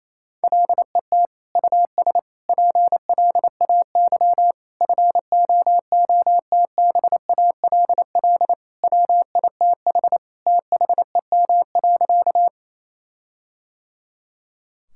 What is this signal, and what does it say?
Morse code, heard as a keyed tone, about 28 wpm, 700 Hz. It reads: LET US PLAY FOOTBALL WITH THEM.
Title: Sentences from Top 1600 Words 28wpm